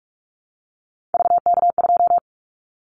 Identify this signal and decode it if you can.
Morse code, heard as a keyed tone, about 46 wpm, 700 Hz. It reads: VK2
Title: Callsign Prefixes for Top 100 DXCC - 45wpm